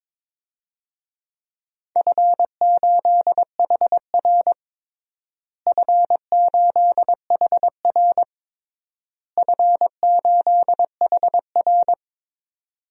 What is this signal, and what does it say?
Morse code, heard as a keyed tone, about 22 wpm, 700 Hz. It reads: F8HR F8HR F8HR
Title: DXCC-100 Call Signs - 4 Characters - Repeated 3x - 22wpm